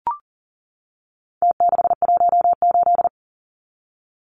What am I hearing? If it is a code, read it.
Morse code, heard as a keyed tone, about 40 wpm, 700 Hz. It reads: T618
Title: Numbers - Contest Style - 4-Digit 40wpm